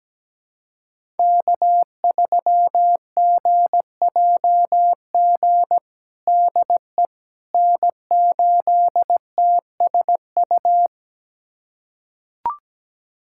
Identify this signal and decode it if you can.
Morse code, heard as a keyed tone, about 17 wpm, 700 Hz. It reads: K3GJG DE N8TSU
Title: USA Callsign DE Callsign - 5 Characters - 17wpm